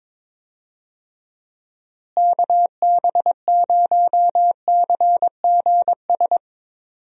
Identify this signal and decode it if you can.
Morse code, heard as a keyed tone, about 22 wpm, 700 Hz. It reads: KB0CGS
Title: USA Call Signs - 6 Characters - Repeated 2x - 22wpm